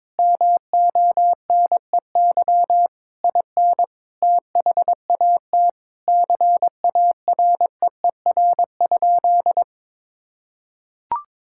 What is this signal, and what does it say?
Morse code, heard as a keyed tone, about 21 wpm, 700 Hz. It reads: MONEY IN THAT CAREER?
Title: Sentences from Top 1000 Words - Review Part 6 - 22wpm